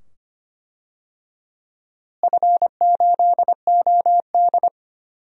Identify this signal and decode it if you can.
Morse code, heard as a keyed tone, about 25 wpm, 700 Hz. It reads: F8OD
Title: DXCC-100 Call Signs - 4 Characters - Repeated 2x - 25wpm